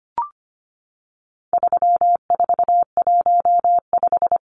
Morse code, heard as a keyed tone, about 25 wpm, 700 Hz. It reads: 3415
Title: Numbers - 4-Digit 25wpm